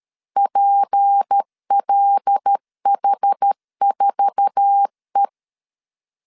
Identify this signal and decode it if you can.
Morse code, heard as a keyed tone, about 13 wpm, 800 Hz. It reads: PLH4E
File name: LEZIONI / 8 YJ45 / 8 YJ4 12wpm 2151